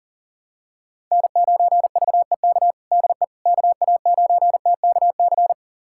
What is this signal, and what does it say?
Morse code, heard as a keyed tone, about 40 wpm, 700 Hz. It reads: N9VEK DE KA9TKC
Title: USA Callsign DE Callsign 40wpm